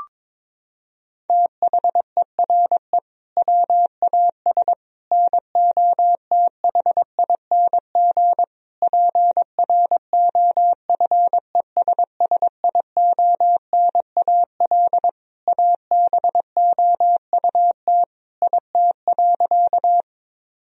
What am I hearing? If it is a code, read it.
Morse code, heard as a keyed tone, about 22 wpm, 700 Hz. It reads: THERE WAS NOTHING PROFESSIONAL ABOUT IT.
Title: Sentences from Top 1000 Words - Review Part 4 - 22wpm